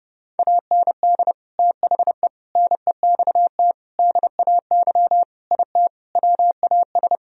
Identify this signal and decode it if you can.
Morse code, heard as a keyed tone, about 30 wpm, 700 Hz. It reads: AND THE NEXT DAY IT WAS
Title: Sentences from Top 1000 Words - Review Part 1 - 30wpm